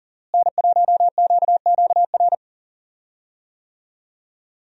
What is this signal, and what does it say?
Morse code, heard as a keyed tone, about 38 wpm, 700 Hz. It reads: N1QQR
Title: USA Call Signs - 5 Characters - Repeated 2x - 40wpm